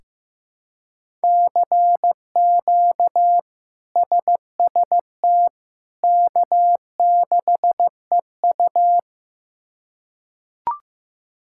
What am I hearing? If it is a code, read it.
Morse code, heard as a keyed tone, about 15 wpm, 700 Hz. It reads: CQ SST K6EU
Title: K1USN SST QSO Practice - 15wpm